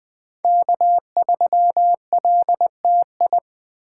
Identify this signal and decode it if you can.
Morse code, heard as a keyed tone, about 21 wpm, 700 Hz. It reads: K3LTI
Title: USA Call Signs - 5 Characters - 3x Character Spacing - 20wpm